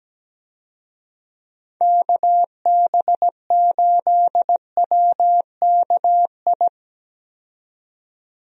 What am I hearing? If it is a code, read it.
Morse code, heard as a keyed tone, about 17 wpm, 700 Hz. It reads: KB8WKI
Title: USA Call Signs - 6 Characters - 17wpm